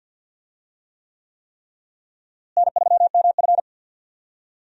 Morse code, heard as a keyed tone, about 47 wpm, 700 Hz. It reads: N3MF
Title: USA Call Signs - 4 Characters - Repeated 3x - 50wpm